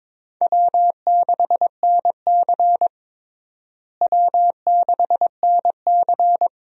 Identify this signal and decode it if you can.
Morse code, heard as a keyed tone, about 22 wpm, 700 Hz. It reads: W6NC W6NC
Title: USA Call Signs - 4 Characters - Repeated 3x - 22wpm